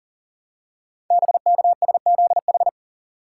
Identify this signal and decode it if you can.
Morse code, heard as a keyed tone, about 42 wpm, 700 Hz. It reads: BKSZH
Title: Code Groups - 5 Letters - 40wpm